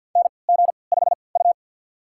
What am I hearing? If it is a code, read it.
Morse code, heard as a keyed tone, about 51 wpm, spaced as for 28 wpm, 700 Hz. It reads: NG5U
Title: USA Call Signs - 4 Characters - 3x Character Spacing - 50wpm